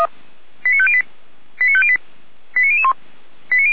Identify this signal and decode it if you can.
Five-tone selective call (ZVEI), heard as quick runs of five tones, tones of about 80 ms. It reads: 89508 89508 89001